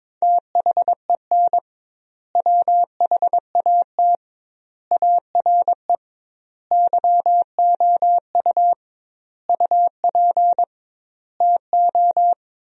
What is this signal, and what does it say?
Morse code, heard as a keyed tone, about 22 wpm, 700 Hz. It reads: THEN WHAT ARE YOU UP TO
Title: 2x Word-Spacing - Sentences from Top 100 Words - 22wpm